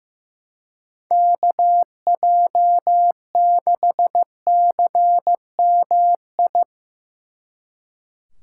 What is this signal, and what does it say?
Morse code, heard as a keyed tone, about 17 wpm, 700 Hz. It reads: KJ6CMI
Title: Speed Racing - USA Call Signs 15wpm